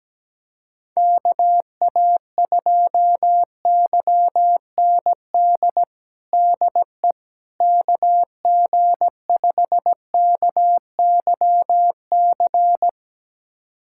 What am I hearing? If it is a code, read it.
Morse code, heard as a keyed tone, about 17 wpm, 700 Hz. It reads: KA2YND DE KG5KYC
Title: USA Callsign DE Callsign - 6 Characters - 17wpm